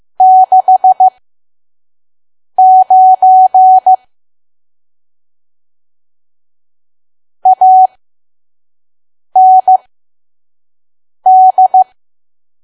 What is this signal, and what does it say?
Morse code, heard as a keyed tone, about 15 wpm, spaced as for 5 wpm, 750 Hz. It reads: 69 AND
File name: Archive / 5 WPM files / PM